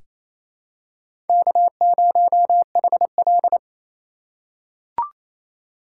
Morse code, heard as a keyed tone, about 28 wpm, 700 Hz. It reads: K0HL
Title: USA Call Signs - 4 Characters - 4x Character Spacing - 28wpm